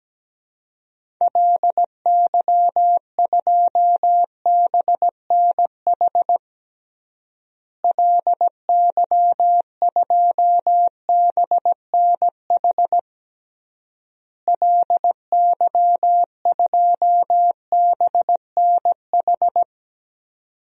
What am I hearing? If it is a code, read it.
Morse code, heard as a keyed tone, about 17 wpm, 700 Hz. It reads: LY2BNH LY2BNH LY2BNH
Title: DXCC-100 Call Signs - 6 Characters - Repeated 3x - 17wpm